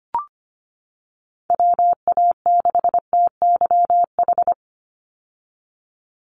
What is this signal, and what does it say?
Morse code, heard as a keyed tone, about 25 wpm, 700 Hz. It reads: WA6TYH
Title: USA Call Signs 25wpm